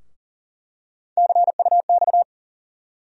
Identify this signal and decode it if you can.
Morse code, heard as a keyed tone, about 40 wpm, 700 Hz. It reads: CUX
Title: USA Call Sign Postfixes - 3 Letters 40wpm